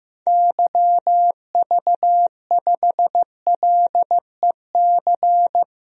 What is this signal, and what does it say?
Morse code, heard as a keyed tone, about 15 wpm, 700 Hz. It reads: YV5LEC
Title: DXCC-100 Call Signs - 6 Characters - Repeated 3x - 15wpm